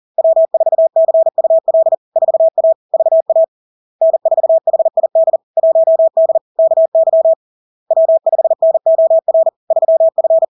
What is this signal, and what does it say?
Morse code, heard as a keyed tone, about 40 wpm, 650 Hz. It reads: W4CUL 4A VA N4HID 1D KY W5NOR 3F
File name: LARK-FD-2022-Log-in-CW-40wpm-650-Hz